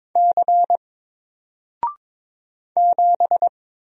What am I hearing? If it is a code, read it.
Morse code, heard as a keyed tone, about 23 wpm, 700 Hz. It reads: C 7
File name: Single-Letter-Number-22wpm